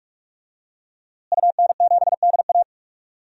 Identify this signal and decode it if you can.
Morse code, heard as a keyed tone, about 44 wpm, 700 Hz. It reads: UN7DA
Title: DXCC-100 Call Signs - 5 Characters - Repeated 2x - 45wpm